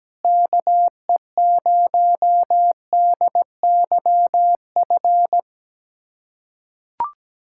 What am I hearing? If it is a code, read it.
Morse code, heard as a keyed tone, about 17 wpm, 700 Hz. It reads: KE0DYF
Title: USA Callsign DE Callsign 17wpm